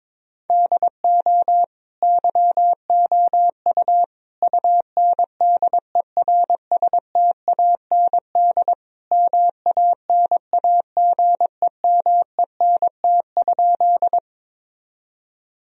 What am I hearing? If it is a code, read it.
Morse code, heard as a keyed tone, about 22 wpm, 700 Hz. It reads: DO YOU UNDERSTAND MANAGEMENT?
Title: Sentences from Top 1000 Words - Review Part 4 - 22wpm